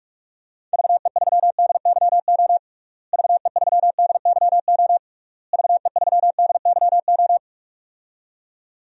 Morse code, heard as a keyed tone, about 41 wpm, 700 Hz. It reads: VE3DYO VE3DYO VE3DYO
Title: DXCC-100 Call Signs - 6 Characters - Repeated 3x - 45wpm